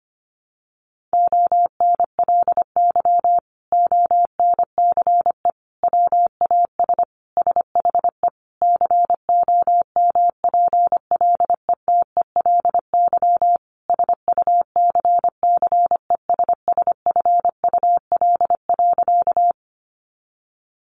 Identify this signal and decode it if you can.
Morse code, heard as a keyed tone, about 25 wpm, 700 Hz. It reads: ONLY ONCE WAS SHE COMPLETELY SUCCESSFUL.
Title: Sentences from Top 1100 Words 25wpm